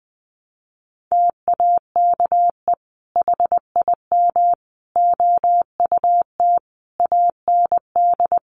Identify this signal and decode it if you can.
Morse code, heard as a keyed tone, about 20 wpm, 700 Hz. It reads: TAKE HIM OUT AND
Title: Sentences from Top 1200 Words 20wpm